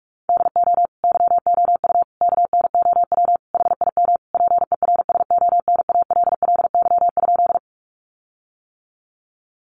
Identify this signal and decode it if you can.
Morse code, heard as a keyed tone, about 44 wpm, 700 Hz. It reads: DO YOU KNOW HIM PERSONALLY?
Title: Sentences from Top 1400 Words 45wpm